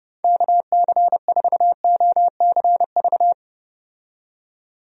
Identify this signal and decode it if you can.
Morse code, heard as a keyed tone, about 29 wpm, 700 Hz. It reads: KC4OCV
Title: USA Call Signs - 6 Characters - 30wpm